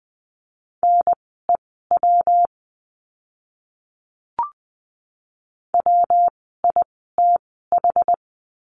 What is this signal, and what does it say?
Morse code, heard as a keyed tone, about 20 wpm, spaced as for 14 wpm, 700 Hz. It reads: NEW WITH
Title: 139 Lesson 23f - Letter f - Words 20wpm